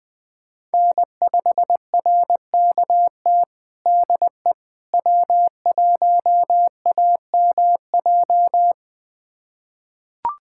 Morse code, heard as a keyed tone, about 20 wpm, 700 Hz. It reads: N5RKT DE W1AMJ
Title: USA Callsign DE Callsign - 5 Characters - 20wpm